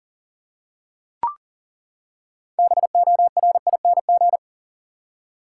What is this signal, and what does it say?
Morse code, heard as a keyed tone, about 40 wpm, 700 Hz. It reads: BORING
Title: Six Letter Words - 40wpm